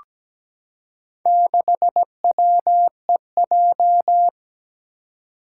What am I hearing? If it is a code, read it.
Morse code, heard as a keyed tone, about 17 wpm, 700 Hz. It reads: 6WEJ
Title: Four Challenging Letter-Number Combinations 17wpm